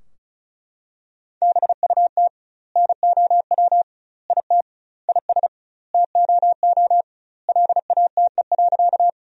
Morse code, heard as a keyed tone, about 36 wpm, 700 Hz. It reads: BUT NOW IT IS TOO LATE.
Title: 2x Word-Spacing - Sentences from Top 300 Words - 35wpm